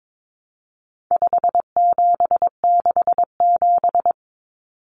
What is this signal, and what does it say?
Morse code, heard as a keyed tone, about 22 wpm, 700 Hz. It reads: 5767
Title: Common Cut Numbers - 4-Digit 22wpm